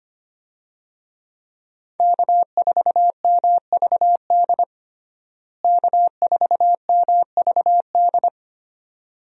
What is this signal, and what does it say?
Morse code, heard as a keyed tone, about 25 wpm, 700 Hz. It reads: K4MVD K4MVD
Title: USA Call Signs - 5 Characters - Repeated 2x - 25wpm